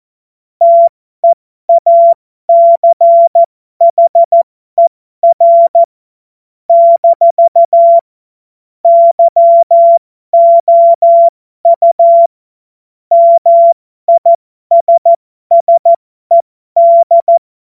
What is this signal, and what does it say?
Morse code, heard as a keyed tone, about 14 wpm, spaced as for 12 wpm, 680 Hz. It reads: TEACHER - YOU MISSED
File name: Morsekurs / Witze